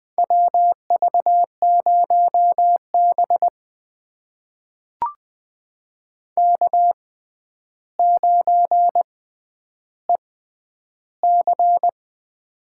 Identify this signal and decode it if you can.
Morse code, heard as a keyed tone, about 21 wpm, spaced as for 10 wpm, 700 Hz. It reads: WV0B K9EC
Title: USA Call Signs - 4 Characters - 6x Character Spacing - 20wpm